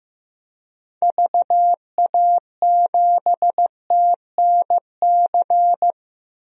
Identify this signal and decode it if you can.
Morse code, heard as a keyed tone, about 15 wpm, 700 Hz. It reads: VA7TNC
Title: DXCC-100 Call Signs - 6 Characters - Repeated 3x - 15wpm